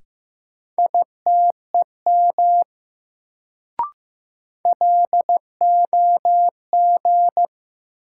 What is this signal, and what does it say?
Morse code, heard as a keyed tone, about 15 wpm, 700 Hz. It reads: ITEM LOG
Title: Sentences from Top 1400 Words - New Word Drill Alt - 15wpm